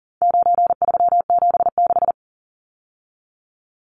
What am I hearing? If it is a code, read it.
Morse code, heard as a keyed tone, about 40 wpm, 700 Hz. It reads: 9376
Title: Numbers - 4-Digit 40wpm